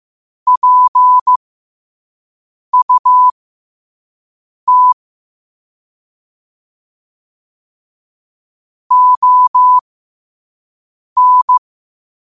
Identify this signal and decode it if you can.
Morse code, heard as a keyed tone, about 15 wpm, spaced as for 5 wpm, 1000 Hz. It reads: PUT ON
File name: qotdfiles / 5wpm